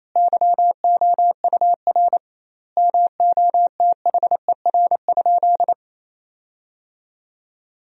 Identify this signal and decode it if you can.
Morse code, heard as a keyed tone, about 28 wpm, 700 Hz. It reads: YOUR MOTHER?
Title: 2x Word-Spacing - Sentences from Top 200 Words - 28wpm